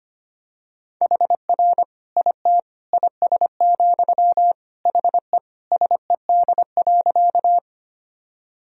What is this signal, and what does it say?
Morse code, heard as a keyed tone, about 25 wpm, 700 Hz. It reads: HR IT IS, HE SED.
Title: Sentences from Top 200 Words and Abbreviations 25wpm